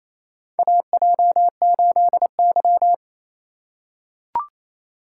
Morse code, heard as a keyed tone, about 28 wpm, 700 Hz. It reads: AJ8Y
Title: USA Call Signs - 4 Characters - 2x Character Spacing - 28wpm